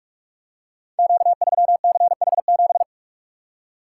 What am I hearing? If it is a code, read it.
Morse code, heard as a keyed tone, about 44 wpm, 690 Hz. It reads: Q3CH7
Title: Code Groups - 5 Letter:Numbers - 45wpm